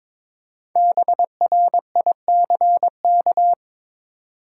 Morse code, heard as a keyed tone, about 22 wpm, 700 Hz. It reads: BRICK
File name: 5 Letter Words - First 3 Letters Given - 22wpm